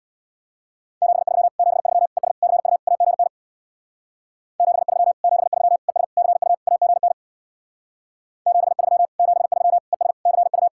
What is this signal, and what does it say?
Morse code, heard as a keyed tone, about 62 wpm, 700 Hz. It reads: 73 73 ES CU AGN 73 73 ES CU AGN 73 73 ES CU
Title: Speed-Racing Common QSO Phrases - 50wpm